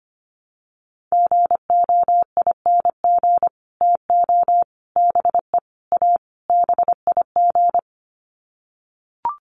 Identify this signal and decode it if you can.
Morse code, heard as a keyed tone, about 25 wpm, 700 Hz. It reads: GOING TO BE A BIG
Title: Sets of 5 Words Encore2 25wpm